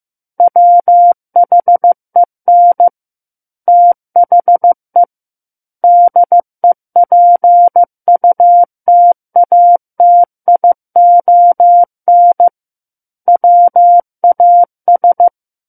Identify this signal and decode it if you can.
Morse code, heard as a keyed tone, about 15 wpm, 700 Hz. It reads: WHEN THE DEPUTATION WAS
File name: Book One / 06-Chapter 6 - 15 WPM